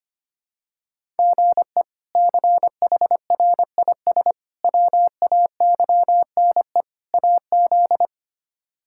Morse code, heard as a keyed tone, about 25 wpm, 700 Hz. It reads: GE CHRIS WAYNE AZ
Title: K1USN SST QSO Practice - 25wpm